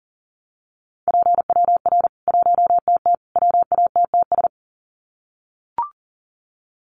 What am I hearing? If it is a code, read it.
Morse code, heard as a keyed tone, about 40 wpm, 700 Hz. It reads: PWR 1TT WATTS
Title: Common QSO Phrases 40wpm